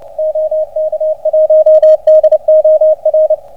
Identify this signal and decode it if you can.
Morse code, heard as a keyed tone, about 28 wpm, 620 Hz, with radio noise. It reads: OK1DOR